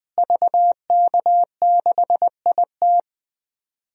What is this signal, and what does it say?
Morse code, heard as a keyed tone, about 20 wpm, 700 Hz. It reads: VK6IT
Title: DXCC-100 Call Signs - 5 Characters - Repeated 3x - 20wpm